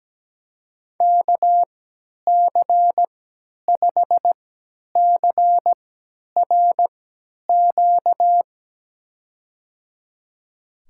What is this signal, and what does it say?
Morse code, heard as a keyed tone, about 17 wpm, spaced as for 10 wpm, 700 Hz. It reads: KC5CRQ
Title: USA Call Signs - 6 Characters - 3x Character Spacing - 17wpm